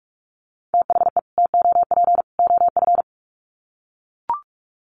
Morse code, heard as a keyed tone, about 44 wpm, 700 Hz. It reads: THE TOP OF
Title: Sets of 3 Words 45wpm